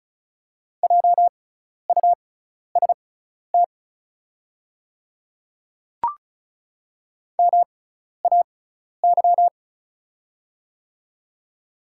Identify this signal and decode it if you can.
Morse code, heard as a keyed tone, about 34 wpm, spaced as for 12 wpm, 690 Hz. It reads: JUST MAY
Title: 6x Character Spacing - Top 200 Words - 35wpm